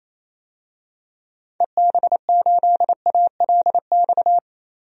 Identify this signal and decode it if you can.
Morse code, heard as a keyed tone, about 28 wpm, 700 Hz. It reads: EB8ALX
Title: DXCC-100 Call Signs - 6 Characters - 28wpm